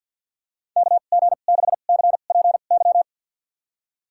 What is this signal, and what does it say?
Morse code, heard as a keyed tone, about 52 wpm, spaced as for 36 wpm, 700 Hz. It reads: KG6CPY
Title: USA Call Signs - 2x Character Spacing - 50wpm